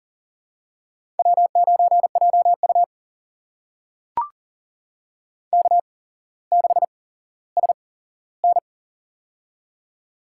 Morse code, heard as a keyed tone, about 40 wpm, spaced as for 17 wpm, 700 Hz. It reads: W9JU K6SN
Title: USA Call Signs - 4 Characters - 8x Character Spacing - 40wpm